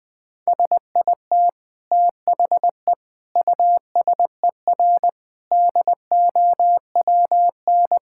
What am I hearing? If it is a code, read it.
Morse code, heard as a keyed tone, about 20 wpm, 700 Hz. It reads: SIT THE USER DOWN
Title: Sentences from Top 1000 Words - Review Part 5 - 20wpm